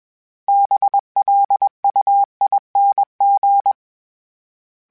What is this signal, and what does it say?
Morse code, heard as a keyed tone, about 21 wpm, 800 Hz. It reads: BLUING
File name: Special INGs / Special INGs-mp3s / Sngl Ing @20